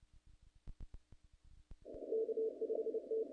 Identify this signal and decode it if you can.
Morse code, heard as a keyed tone, about 28 wpm, 450 Hz. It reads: K5T
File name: contests / naqp-cw-k5tr-op-k5pi- -1900